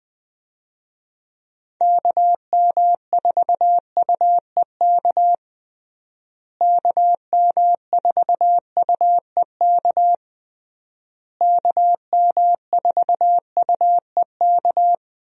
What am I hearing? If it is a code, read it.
Morse code, heard as a keyed tone, about 21 wpm, 700 Hz. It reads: KM4UEK KM4UEK KM4UEK
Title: USA Call Signs - 6 Characters - Repeated 3x - 20wpm